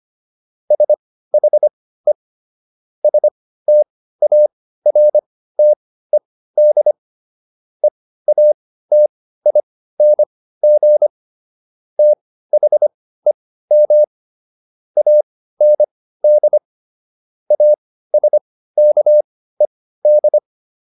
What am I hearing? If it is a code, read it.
Morse code, heard as a keyed tone, about 25 wpm, spaced as for 15 wpm, 600 Hz. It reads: SHE STARTED EATING THEM AND ASKED